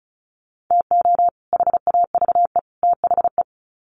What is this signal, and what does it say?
Morse code, heard as a keyed tone, about 34 wpm, 700 Hz. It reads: TO HAVE THE
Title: Sets of 3 Words 35wpm